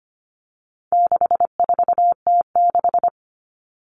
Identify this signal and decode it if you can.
Morse code, heard as a keyed tone, about 25 wpm, 700 Hz. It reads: 64T6
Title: Common Cut Numbers - 4-Digit 25wpm